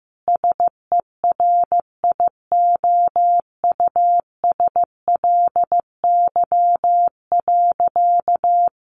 Morse code, heard as a keyed tone, about 14 wpm, 700 Hz. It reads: SERIOUSLY.
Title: Sentences from Top 1200 Words 15wpm